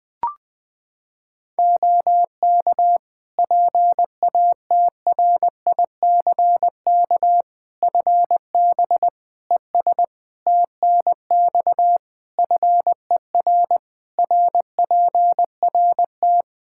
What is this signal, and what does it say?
Morse code, heard as a keyed tone, about 20 wpm, 700 Hz. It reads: OK PATRICK FB ES TNX FER RPRT
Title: Practice QSO - Exchange 2 - Info - 20wpm